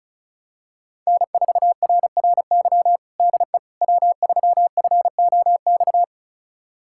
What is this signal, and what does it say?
Morse code, heard as a keyed tone, about 36 wpm, 700 Hz. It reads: N4RRY DE W3FOX
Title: USA Callsign DE Callsign - 5 Characters - 35wpm